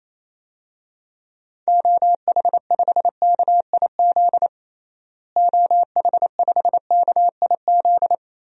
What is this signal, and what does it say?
Morse code, heard as a keyed tone, about 28 wpm, 700 Hz. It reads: OH5KIZ OH5KIZ
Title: DXCC-100 Call Signs - 6 Characters - Repeated 3x - 28wpm